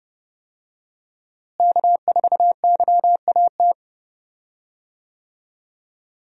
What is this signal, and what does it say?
Morse code, heard as a keyed tone, about 31 wpm, 700 Hz. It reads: K4YAT
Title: USA Call Signs - 5 Characters - Repeated 2x - 30wpm